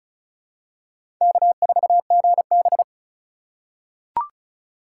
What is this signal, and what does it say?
Morse code, heard as a keyed tone, about 34 wpm, 700 Hz. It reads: K4GB
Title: USA Call Signs - 4 Characters - 4x Character Spacing - 35wpm